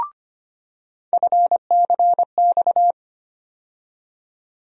Morse code, heard as a keyed tone, about 25 wpm, 700 Hz. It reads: FCX
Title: USA Call Sign Postfixes - 3 Letters 25wpm